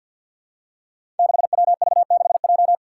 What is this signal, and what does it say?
Morse code, heard as a keyed tone, about 46 wpm, 700 Hz. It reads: 6WV6J
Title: Five Challenging Letter-Number Combinations 50wpm